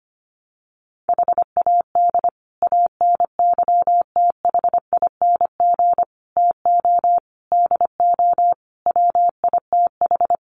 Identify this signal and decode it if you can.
Morse code, heard as a keyed tone, about 25 wpm, 700 Hz. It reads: HAD ANYTHING TO DO WITH
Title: Sets of 5 Words Encore 25wpm